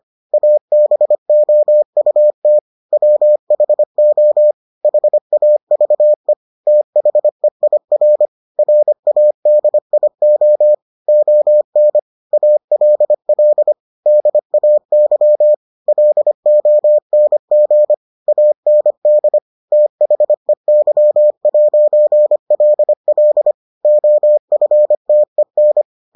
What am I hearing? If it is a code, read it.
Morse code, heard as a keyed tone, about 25 wpm, 600 Hz. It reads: ABOUT WHO HAVE THEIR RADIO ON ALL DAY LONG AND THEY'LL OFTEN